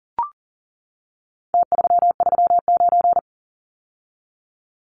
Morse code, heard as a keyed tone, about 40 wpm, 700 Hz. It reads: T339
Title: Numbers - Contest Style - 4-Digit 40wpm